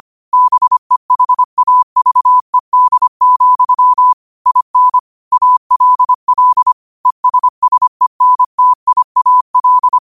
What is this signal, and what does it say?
Morse code, heard as a keyed tone, about 25 wpm, 1000 Hz. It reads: BEHAVED, IN ALL ESSENTIAL
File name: qotdfiles / 25wpm